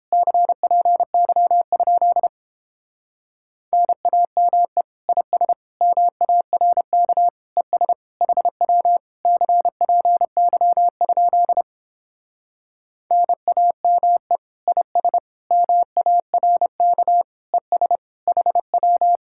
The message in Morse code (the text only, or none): CPY? NAME IS MARK ES HW CPY? NAME IS MARK ES HW